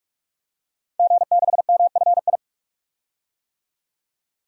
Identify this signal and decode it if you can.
Morse code, heard as a keyed tone, about 47 wpm, 700 Hz. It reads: G6MFI
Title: DXCC-100 Call Signs - 5 Characters - 45wpm